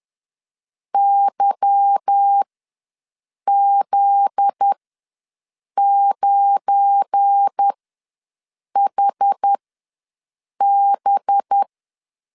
Morse code, heard as a keyed tone, about 10 wpm, spaced as for 6 wpm, 800 Hz. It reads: YZ9HB